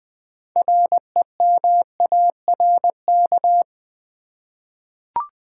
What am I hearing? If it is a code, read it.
Morse code, heard as a keyed tone, about 21 wpm, 700 Hz. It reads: REMARK
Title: 6 Letter Words - First 3 Letters Given - 20wpm